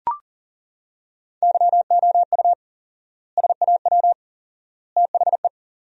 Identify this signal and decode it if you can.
Morse code, heard as a keyed tone, about 38 wpm, 700 Hz. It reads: YOU SAW THE
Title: 4x Word-Spacing - Sentences from Top 400 Words - 40wpm